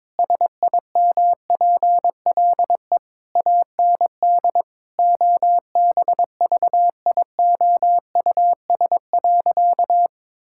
Morse code, heard as a keyed tone, about 22 wpm, 700 Hz. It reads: SIMPLE AND OBVIOUS.
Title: Sentences from Top 1600 Words 22wpm